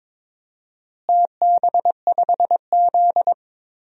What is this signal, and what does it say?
Morse code, heard as a keyed tone, about 22 wpm, 700 Hz. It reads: TB5Z
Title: DXCC-100 Call Signs - 4 Characters - Repeated 2x - 22wpm